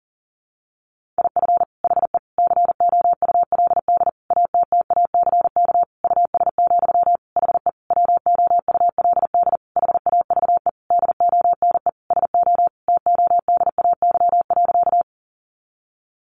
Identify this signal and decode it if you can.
Morse code, heard as a keyed tone, about 40 wpm, 700 Hz. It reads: IF HE COULD ATTACK US, HE WOULD HAVE DONE SO TODAY.
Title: Sentences from Top 1500 Words 40wpm